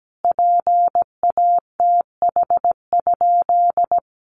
Morse code, heard as a keyed tone, about 17 wpm, 700 Hz. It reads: PATH?
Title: Sentences from Top 1400 Words 17wpm